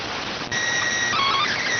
SSTV (slow-scan television), PD 290.